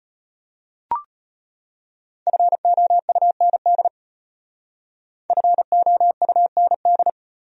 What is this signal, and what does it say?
Morse code, heard as a keyed tone, about 32 wpm, 700 Hz. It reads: FOUND FOUND